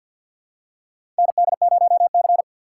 Morse code, heard as a keyed tone, about 53 wpm, 700 Hz. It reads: ND0C
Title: USA Call Signs - 4 Characters - 3x Character Spacing - 50wpm